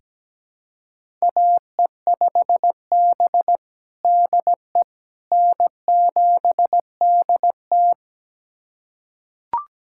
Morse code, heard as a keyed tone, about 17 wpm, 700 Hz. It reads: AE5B DE N7DT
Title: USA Callsign DE Callsign - 4 Characters - 17wpm